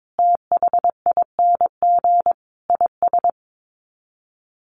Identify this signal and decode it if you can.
Morse code, heard as a keyed tone, about 22 wpm, 700 Hz. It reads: THING IS
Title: Sets of 5 Words Encore 22wpm